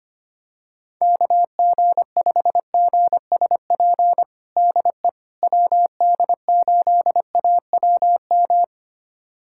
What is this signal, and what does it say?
Morse code, heard as a keyed tone, about 25 wpm, 700 Hz. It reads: KG5GSP DE WD8AWM
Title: USA Callsign DE Callsign 25wpm